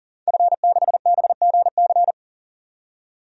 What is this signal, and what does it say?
Morse code, heard as a keyed tone, about 37 wpm, 700 Hz. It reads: F6BGC
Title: DXCC-100 Call Signs - 5 Characters - 40wpm